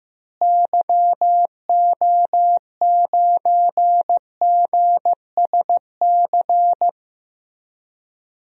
Morse code, heard as a keyed tone, about 15 wpm, 700 Hz. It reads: YO9GSC
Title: DXCC-100 Call Signs - 6 Characters - Repeated 3x - 15wpm